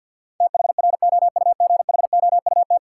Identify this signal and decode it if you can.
Morse code, heard as a keyed tone, about 53 wpm, 700 Hz. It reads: THROUGHOUT
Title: Sentences from Top 800 Words - New Word Drill - 50wpm